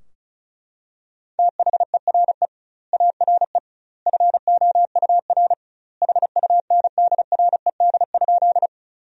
Morse code, heard as a keyed tone, about 35 wpm, 700 Hz. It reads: THERE ARE FOUR HUNDRED?
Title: 2x Word-Spacing - Sentences from Top 400 Words - 35wpm